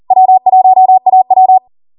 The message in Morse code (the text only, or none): W1AW